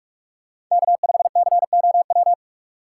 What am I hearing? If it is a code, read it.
Morse code, heard as a keyed tone, about 44 wpm, 700 Hz. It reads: K5COW
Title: USA Call Signs - 5 Characters - 6x Character Spacing - 45wpm